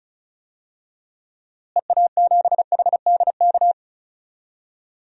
Morse code, heard as a keyed tone, about 36 wpm, 700 Hz. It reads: EA7HDK
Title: DXCC-100 Call Signs - 6 Characters - Repeated 3x - 35wpm